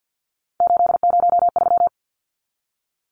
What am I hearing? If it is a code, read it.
Morse code, heard as a keyed tone, about 50 wpm, 700 Hz. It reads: 803
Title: Numbers - 3-Digit 50wpm